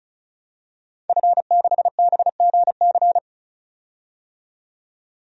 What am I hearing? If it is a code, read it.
Morse code, heard as a keyed tone, about 36 wpm, 700 Hz. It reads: F6BGC
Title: DXCC-100 Call Signs - 5 Characters - 35wpm